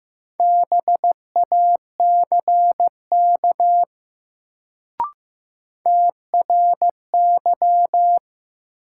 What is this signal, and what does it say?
Morse code, heard as a keyed tone, about 15 wpm, 700 Hz. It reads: BACK TRY
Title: Sets of 2 Words - New Word Drill - 15wpm